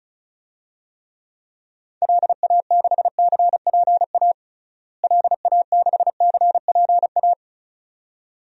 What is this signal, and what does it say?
Morse code, heard as a keyed tone, about 34 wpm, 700 Hz. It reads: LA6CPA LA6CPA
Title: DXCC-100 Call Signs - 6 Characters - Repeated 2x - 35wpm